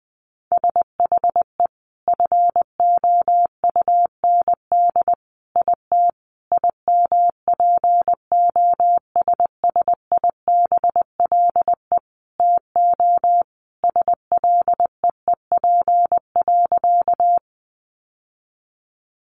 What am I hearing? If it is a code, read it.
Morse code, heard as a keyed tone, about 20 wpm, 700 Hz. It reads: SHE FOUND IT IMPOSSIBLE TO SLEEP.
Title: Sentences from Top 1100 Words 20wpm